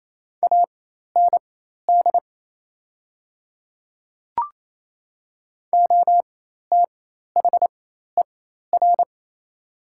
Morse code, heard as a keyed tone, about 28 wpm, spaced as for 13 wpm, 690 Hz. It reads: AND OTHER
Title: 4x Character Spacing - Top 200 Words - 28wpm